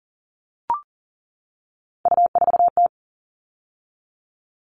Morse code, heard as a keyed tone, about 40 wpm, 700 Hz. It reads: U4T
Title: All Cut Numbers - 3-Digit 40wpm